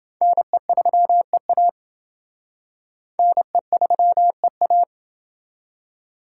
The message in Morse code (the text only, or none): NE3EA NE3EA